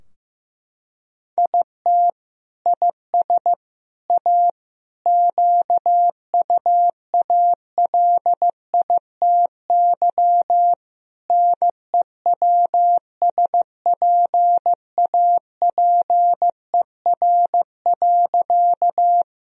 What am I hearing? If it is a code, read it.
Morse code, heard as a keyed tone, about 15 wpm, 700 Hz. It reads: IT IS A QUALITY NEWSPAPER.
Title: Sentences from Top 1000 Words - Review Part 4 - 15wpm